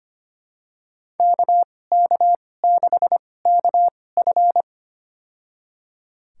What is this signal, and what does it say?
Morse code, heard as a keyed tone, about 25 wpm, spaced as for 18 wpm, 700 Hz. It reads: KK6KF
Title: USA Call Signs - 5 Characters - 2x Character Spacing - 25wpm